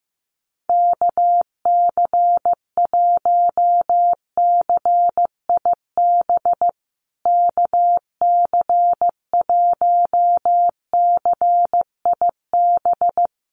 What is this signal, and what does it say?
Morse code, heard as a keyed tone, about 15 wpm, 700 Hz. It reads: KC1CIB KC1CIB
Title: USA Call Signs Repeated 3x 15wpm